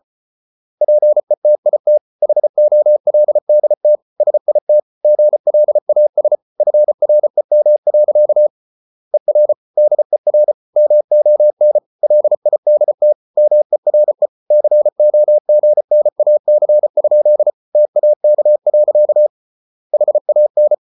Morse code, heard as a keyed tone, about 34 wpm, 600 Hz. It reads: PETIT HOLDT SIT GLAS FREM. ER DER MON LIDT MERE COGNAC? TAK. HAN
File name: Ferie QRQ 34wpm